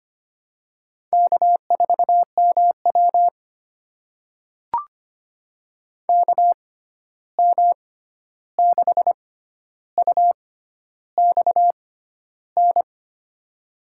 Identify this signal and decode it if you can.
Morse code, heard as a keyed tone, about 25 wpm, spaced as for 11 wpm, 700 Hz. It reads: K4MW KM6UXN
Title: USA Call Signs - 6x Character Spacing - 25wpm